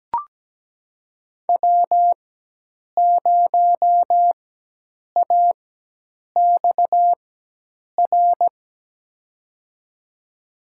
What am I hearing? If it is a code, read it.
Morse code, heard as a keyed tone, about 17 wpm, spaced as for 8 wpm, 700 Hz. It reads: W0AXR
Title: USA Call Signs - 4x Character Spacing - 17wpm